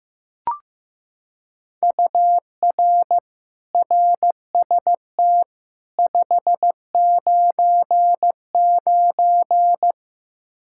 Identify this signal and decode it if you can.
Morse code, heard as a keyed tone, about 15 wpm, 700 Hz. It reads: UR RST 599
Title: Practice QSO - Exchange 1 - Beginning - 15wpm